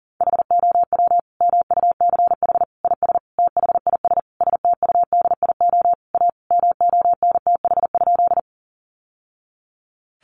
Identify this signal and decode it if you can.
Morse code, heard as a keyed tone, about 40 wpm, 700 Hz. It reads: HOW MUCH IS THIS STUDIO A MONTH?
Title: Sentences from Top 1200 Words 40wpm